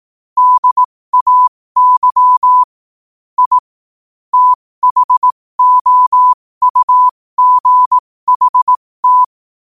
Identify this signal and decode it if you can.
Morse code, heard as a keyed tone, about 18 wpm, spaced as for 15 wpm, 1000 Hz. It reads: DAY I THOUGHT
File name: qotdfiles / 15wpm